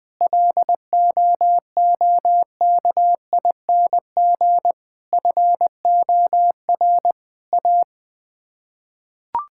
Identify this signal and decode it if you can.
Morse code, heard as a keyed tone, about 20 wpm, 700 Hz. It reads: LOOKING FOR A
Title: Sets of 6 Words - 20wpm